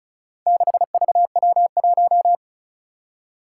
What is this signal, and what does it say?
Morse code, heard as a keyed tone, about 36 wpm, 700 Hz. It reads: 6VW1
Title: Four Challenging Letter-Number Combinations 35wpm